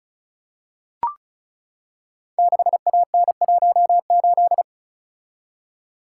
Morse code, heard as a keyed tone, about 36 wpm, 700 Hz. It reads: 6AN18
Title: Code Groups - 5 Letter:Numbers - 35wpm